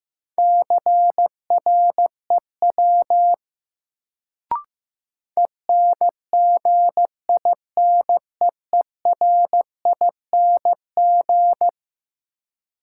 Morse code, heard as a keyed tone, about 15 wpm, 700 Hz. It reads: CREW ENGINEERING
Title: Sentences from Top 1500 Words - New Word Drill Alt - 15wpm